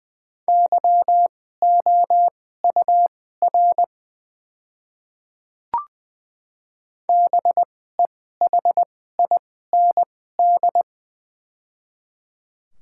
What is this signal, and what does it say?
Morse code, heard as a keyed tone, about 20 wpm, spaced as for 15 wpm, 700 Hz. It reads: YOUR BEHIND
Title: 2x Character Spacing - Top 500 Words - 20wpm